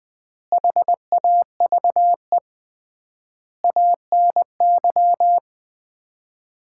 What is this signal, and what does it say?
Morse code, heard as a keyed tone, about 20 wpm, 700 Hz. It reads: HAVE ANY
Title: 3x Word-Spacing - Sets of 4 Words - 20wpm